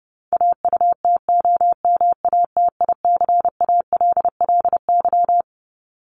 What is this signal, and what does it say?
Morse code, heard as a keyed tone, about 30 wpm, 700 Hz. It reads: AUTOMATICALLY
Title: Top 1751-2000 Words 30wpm